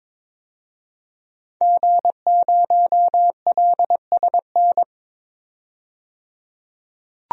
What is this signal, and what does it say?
Morse code, heard as a keyed tone, about 22 wpm, 700 Hz. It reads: G0LSN
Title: DXCC-100 Call Signs - 5 Characters - 22wpm